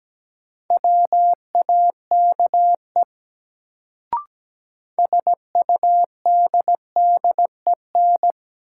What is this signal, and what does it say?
Morse code, heard as a keyed tone, about 17 wpm, 700 Hz. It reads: WAKE SUDDEN
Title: Sets of 4 Words - New Word Drill - 17wpm